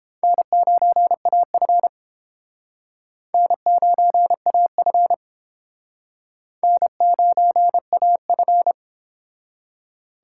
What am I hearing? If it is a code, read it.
Morse code, heard as a keyed tone, about 29 wpm, 700 Hz. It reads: N9AF N9AF N9AF